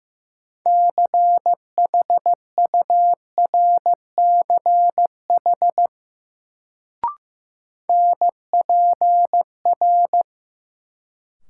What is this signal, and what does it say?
Morse code, heard as a keyed tone, about 15 wpm, 700 Hz. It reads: CHURCH NPR
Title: Sets of 5 Words - New Word Drill - 15wpm